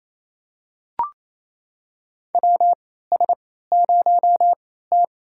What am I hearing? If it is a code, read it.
Morse code, heard as a keyed tone, about 28 wpm, spaced as for 16 wpm, 700 Hz. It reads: WS0T
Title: USA Call Signs - 4 Characters - 3x Character Spacing - 28wpm